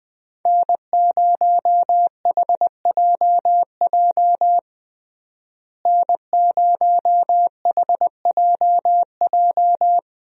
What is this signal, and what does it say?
Morse code, heard as a keyed tone, about 21 wpm, 700 Hz. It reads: N0HJJ N0HJJ
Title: USA Call Signs - 5 Characters - Repeated 2x - 20wpm